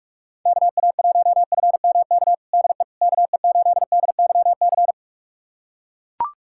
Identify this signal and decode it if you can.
Morse code, heard as a keyed tone, about 41 wpm, 700 Hz. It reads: KA1FMK DE KE8DYC
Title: USA Callsign DE Callsign - 6 Characters - 45wpm